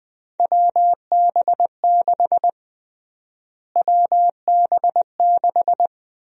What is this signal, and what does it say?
Morse code, heard as a keyed tone, about 20 wpm, 700 Hz. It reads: WB6 WB6
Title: USA Call Sign Prefixes - Repeated 2x - 20wpm